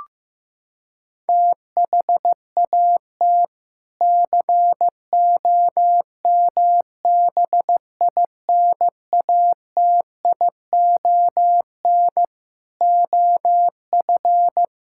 Morse code, heard as a keyed tone, about 15 wpm, 700 Hz. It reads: THAT COMBINATION OF
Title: Sentences from Top 1000 Words - Review Part 6 - 15wpm